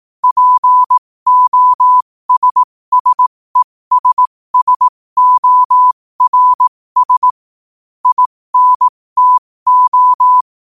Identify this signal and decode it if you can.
Morse code, heard as a keyed tone, about 18 wpm, spaced as for 15 wpm, 1000 Hz. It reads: POSSESSORS INTO
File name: qotdfiles / 15wpm